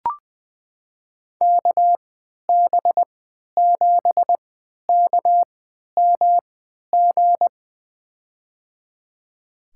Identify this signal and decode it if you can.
Morse code, heard as a keyed tone, about 20 wpm, spaced as for 11 wpm, 700 Hz. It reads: KB7KMG
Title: USA Call Signs - 6 Characters - 3x Character Spacing - 20wpm